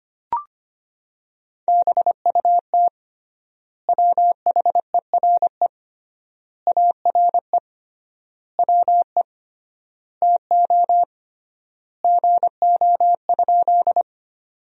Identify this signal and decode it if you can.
Morse code, heard as a keyed tone, about 25 wpm, 700 Hz. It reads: BUT WHERE ARE WE TO GO?
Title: 3x Word-Spacing - Sentences from Top 200 Words - 25wpm